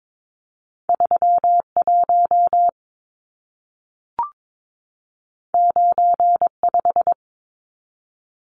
Morse code, heard as a keyed tone, about 22 wpm, 700 Hz. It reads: 31 95
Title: Numbers - 2-Digit 22wpm